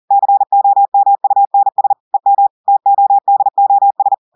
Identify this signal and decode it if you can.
Morse code, heard as a keyed tone, about 40 wpm, 800 Hz. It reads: COMUNS EM TODOS
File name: audios / portugais / portugais texte en clair 2 vitesse 40 mots minutes